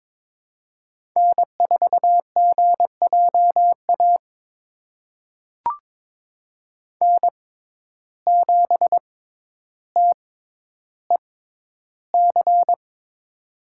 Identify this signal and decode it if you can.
Morse code, heard as a keyed tone, about 23 wpm, spaced as for 11 wpm, 700 Hz. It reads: N4GJA N7TEC
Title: USA Call Signs - 5 Characters - 6x Character Spacing - 22wpm